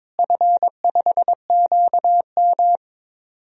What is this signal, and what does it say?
Morse code, heard as a keyed tone, about 21 wpm, 700 Hz. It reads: F5QM
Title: DXCC-100 Call Signs - 4 Characters - Repeated 3x - 22wpm